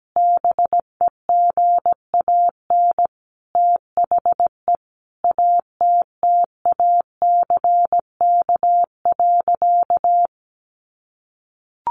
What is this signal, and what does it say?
Morse code, heard as a keyed tone, about 17 wpm, 700 Hz. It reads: BEGAN THE ATTACK.
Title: Sentences from Top 1500 Words 17wpm